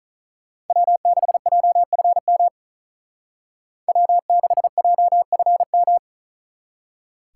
Morse code, heard as a keyed tone, about 44 wpm, 700 Hz. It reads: W6JFM W6JFM
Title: Speed Racing - USA 1x3 Call Signs 35wpm